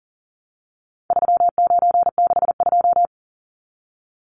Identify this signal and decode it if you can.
Morse code, heard as a keyed tone, about 40 wpm, 700 Hz. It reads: 3962
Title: Numbers - 4-Digit 40wpm